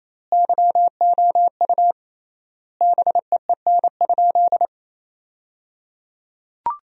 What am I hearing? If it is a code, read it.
Morse code, heard as a keyed tone, about 28 wpm, 700 Hz. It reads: YOU BEEN?
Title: 3x Word-Spacing - Sentences from Top 200 Words - 28wpm